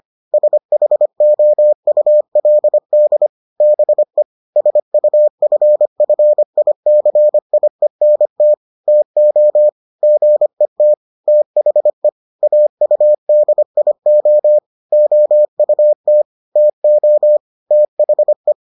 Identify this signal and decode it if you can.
Morse code, heard as a keyed tone, about 25 wpm, 600 Hz. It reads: SHOULD BE SUFFICIENT TO GET THE AUDIO OUT TO THE